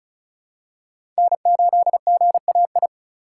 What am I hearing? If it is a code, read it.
Morse code, heard as a keyed tone, about 36 wpm, 700 Hz. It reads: N8GAI
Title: USA Call Signs - 5 Characters - 3x Character Spacing - 35wpm